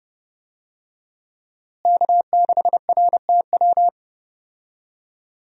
Morse code, heard as a keyed tone, about 30 wpm, 700 Hz. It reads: K6RTW
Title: USA Call Signs - 5 Characters - Repeated 2x - 30wpm